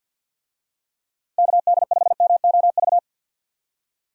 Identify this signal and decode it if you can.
Morse code, heard as a keyed tone, about 51 wpm, 700 Hz. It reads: KD5MOV
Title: USA Call Signs - 6 Characters - 50wpm